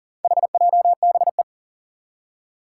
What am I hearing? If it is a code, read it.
Morse code, heard as a keyed tone, about 40 wpm, 700 Hz. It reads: HJBE
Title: Four Challenging Letter-Number Combinations 40wpm